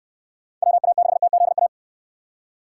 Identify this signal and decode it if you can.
Morse code, heard as a keyed tone, about 63 wpm, 700 Hz. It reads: WA6TZA